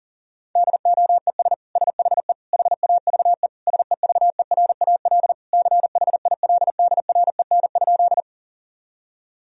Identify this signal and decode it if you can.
Morse code, heard as a keyed tone, about 40 wpm, 700 Hz. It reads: DOES SHE HAVE SEVERAL CHILDREN?
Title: Sentences from Top 1000 Words - Review Part 2 - 40wpm